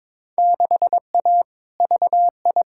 Morse code, heard as a keyed tone, about 22 wpm, 700 Hz. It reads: 6A VI
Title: Field Day Exchanges Encore - 22wpm